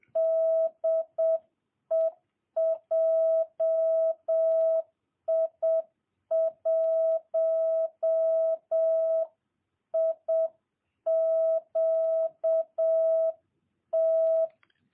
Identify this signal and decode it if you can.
Morse code, heard as a keyed tone, about 7 wpm, 650 Hz. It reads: DE JI1IQT
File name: jh8cbh / 2022cw